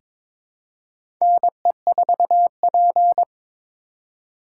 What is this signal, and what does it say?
Morse code, heard as a keyed tone, about 21 wpm, 700 Hz. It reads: NE4P
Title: USA Call Signs - 4 Characters - 2x Character Spacing - 22wpm